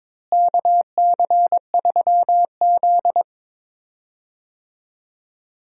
Morse code, heard as a keyed tone, about 21 wpm, 700 Hz. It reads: KC3Z
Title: USA Call Signs - 4 Characters - Repeated 3x - 22wpm